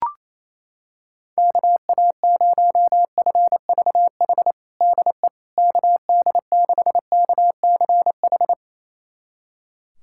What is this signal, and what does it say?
Morse code, heard as a keyed tone, about 28 wpm, 700 Hz. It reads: KA0FVH DE KD6KCH
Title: USA Callsign DE Callsign - 6 Characters - 28wpm